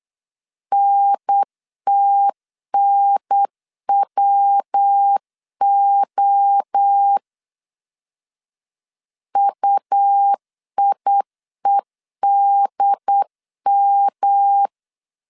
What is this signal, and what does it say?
Morse code, heard as a keyed tone, about 8 wpm, 800 Hz. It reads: NTNWO UIEDM